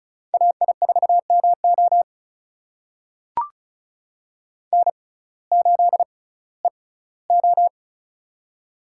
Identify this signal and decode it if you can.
Morse code, heard as a keyed tone, about 34 wpm, spaced as for 19 wpm, 700 Hz. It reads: AI4MO N8EO
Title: USA Call Signs - 6x Character Spacing - 35wpm